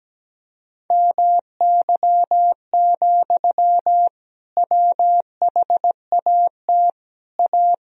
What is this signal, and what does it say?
Morse code, heard as a keyed tone, about 17 wpm, 700 Hz. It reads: MY, WHAT A
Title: Sentences from Top 1000 Words - Review Part 2 - 17wpm